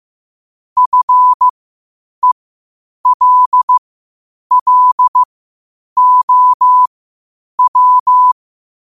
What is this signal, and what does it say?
Morse code, heard as a keyed tone, about 15 wpm, spaced as for 8 wpm, 1000 Hz. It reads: FELLOW
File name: qotdfiles / 8wpm